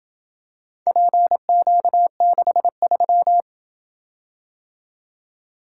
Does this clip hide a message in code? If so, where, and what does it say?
Yes: Morse code from 0.9 to 3.4 s, heard as a keyed tone, about 27 wpm, 700 Hz. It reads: PQ63